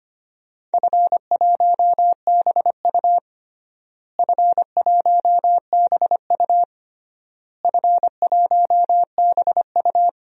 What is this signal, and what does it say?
Morse code, heard as a keyed tone, about 25 wpm, 700 Hz. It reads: F1BU F1BU F1BU
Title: DXCC-100 Call Signs - 4 Characters - Repeated 3x - 25wpm